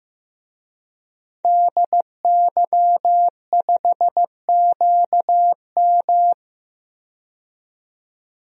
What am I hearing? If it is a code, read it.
Morse code, heard as a keyed tone, about 15 wpm, 700 Hz. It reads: DY5QM
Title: DXCC-100 Call Signs - 5 Characters - 15wpm